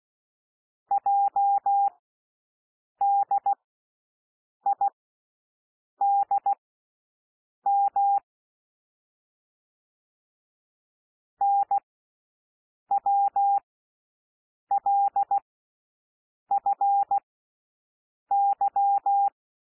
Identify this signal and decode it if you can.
Morse code, heard as a keyed tone, about 16 wpm, spaced as for 6 wpm, 800 Hz. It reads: JDIDM NWLFY
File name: cwmp3 / L1 SOUND LETTERS 16 wpm